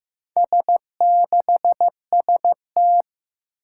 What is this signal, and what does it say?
Morse code, heard as a keyed tone, about 15 wpm, 700 Hz. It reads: S6ST
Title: Four Challenging Letter-Number Combinations 15wpm